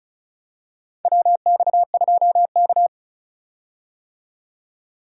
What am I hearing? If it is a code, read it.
Morse code, heard as a keyed tone, about 36 wpm, 700 Hz. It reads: WX2K
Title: USA Call Signs - 4 Characters - 35wpm